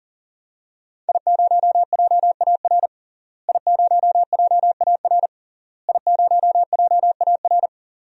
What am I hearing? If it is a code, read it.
Morse code, heard as a keyed tone, about 40 wpm, 700 Hz. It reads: I0JAR I0JAR I0JAR
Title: DXCC-100 Call Signs - 5 Characters - Repeated 3x - 40wpm